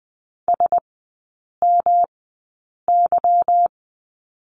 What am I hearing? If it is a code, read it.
Morse code, heard as a keyed tone, about 20 wpm, spaced as for 8 wpm, 700 Hz. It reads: SMY
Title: 145 Lesson 24e - Letter y - Three Characters 20wpm